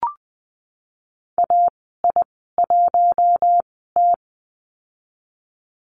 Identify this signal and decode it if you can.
Morse code, heard as a keyed tone, about 20 wpm, spaced as for 15 wpm, 700 Hz. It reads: AI1T
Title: 088 Lesson 15g - Number 5 - Call Signs 20wpm